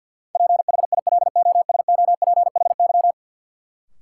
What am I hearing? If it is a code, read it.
Morse code, heard as a keyed tone, about 53 wpm, 700 Hz. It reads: PHILOSOPHY
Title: Sentences from Top 900 Words - New Word Drill - 50wpm